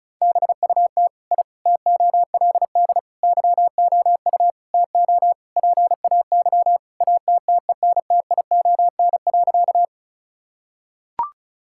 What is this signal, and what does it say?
Morse code, heard as a keyed tone, about 36 wpm, 700 Hz. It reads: BUT I TOLD YOU TO PAY ATTENTION.
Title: Sentences from Top 1000 Words - Review Part 6 - 35wpm